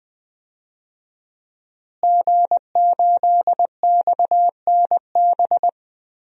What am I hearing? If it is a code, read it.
Morse code, heard as a keyed tone, about 20 wpm, 700 Hz. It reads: G8XNB
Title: DXCC-100 Call Signs - 5 Characters - Repeated 2x - 20wpm